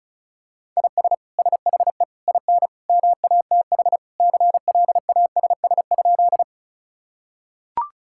Morse code, heard as a keyed tone, about 36 wpm, 700 Hz. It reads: IS SHE IN MATH CLASS?
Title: Sentences from Top 1000 Words - Review Part 6 - 35wpm